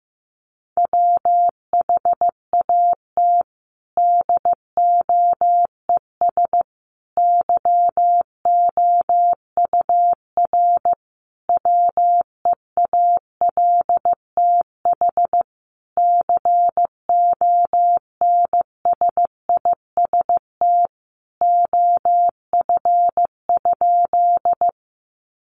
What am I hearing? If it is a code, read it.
Morse code, heard as a keyed tone, about 15 wpm, 700 Hz. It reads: WHAT DOES YOUR WEALTH CONSIST OF?
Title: Sentences from Top 1400 Words 15wpm